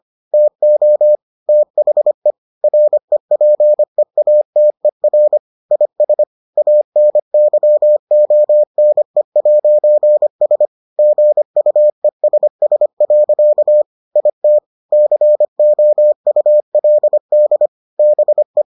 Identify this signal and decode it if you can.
Morse code, heard as a keyed tone, about 25 wpm, 600 Hz. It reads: TO THE REPEATER IS ANYONE'S GUESS. IT COULD BE